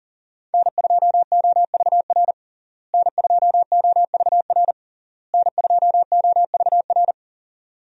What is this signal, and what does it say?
Morse code, heard as a keyed tone, about 40 wpm, 700 Hz. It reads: N2OVR N2OVR N2OVR
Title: USA Call Signs - 5 Characters - Repeated 3x - 40wpm